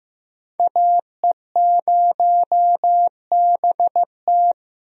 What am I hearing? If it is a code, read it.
Morse code, heard as a keyed tone, about 16 wpm, 700 Hz. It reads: AE0BT
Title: USA Call Signs - 5 Characters - Repeated 2x - 15wpm